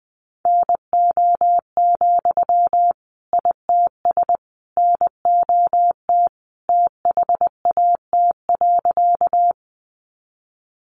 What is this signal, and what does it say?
Morse code, heard as a keyed tone, about 20 wpm, 700 Hz. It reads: NO, ITS NOT THAT.
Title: Sentences from Top 600 Words 20wpm